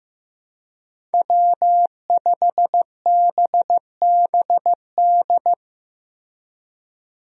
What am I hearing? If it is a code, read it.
Morse code, heard as a keyed tone, about 15 wpm, 700 Hz. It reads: W5BBD
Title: USA Call Signs - 5 Characters - Repeated 2x - 15wpm